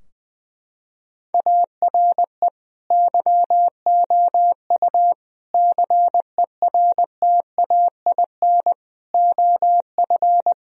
Morse code, heard as a keyed tone, about 20 wpm, 700 Hz. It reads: ARE YOU CERTAIN OF
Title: Sentences from Top 1000 Words - Review Part 2 - 20wpm